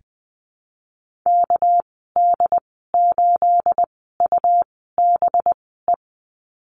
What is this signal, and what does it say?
Morse code, heard as a keyed tone, about 20 wpm, spaced as for 14 wpm, 700 Hz. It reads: KD8UBE
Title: 238 Lesson 38g - Letter q - Call Signs 20wpm